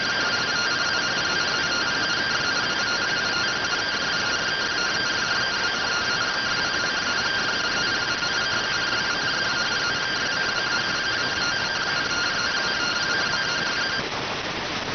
Two-tone teleprinter signal, about 170 Hz shift.